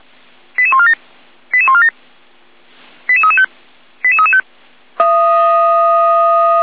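Five-tone selective call (ZVEI), heard as quick runs of five tones, tones of about 70 ms. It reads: 88147 88147 88335 88335
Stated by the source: ZVEI five-tone selective call 88397 received by SDR.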